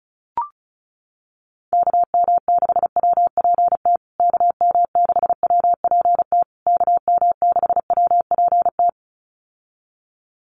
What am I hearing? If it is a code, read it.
Morse code, heard as a keyed tone, about 34 wpm, 700 Hz. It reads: KM6WPT KM6WPT KM6WPT
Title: USA Call Signs Repeated 3x 35wpm